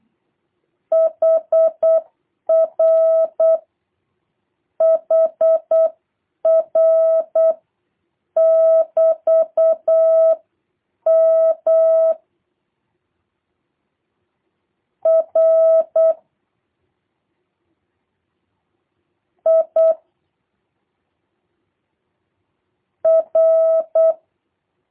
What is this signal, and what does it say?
Morse code, heard as a keyed tone, about 8 wpm, 650 Hz. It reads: HR HR <BT> M R I R
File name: jh8cbh / 2023cw / review1-2